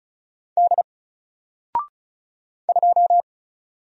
Morse code, heard as a keyed tone, about 35 wpm, 700 Hz. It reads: D 2
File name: Single-Letter-Number-35wpm